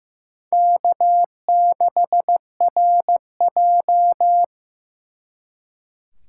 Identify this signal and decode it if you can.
Morse code, heard as a keyed tone, about 18 wpm, 700 Hz. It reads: K6RJ
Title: Speed Racing - USA 1x2 Call Signs 15wpm